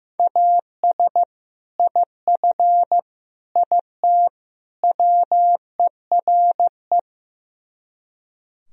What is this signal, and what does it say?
Morse code, heard as a keyed tone, about 15 wpm, 700 Hz. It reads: AS IF IT WERE
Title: Speed-Racing - Sets of 4 Words 15wpm